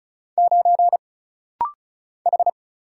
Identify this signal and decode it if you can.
Morse code, heard as a keyed tone, about 36 wpm, 700 Hz. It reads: 9 H
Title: Single Letter-Number - Rapid-Fire 35wpm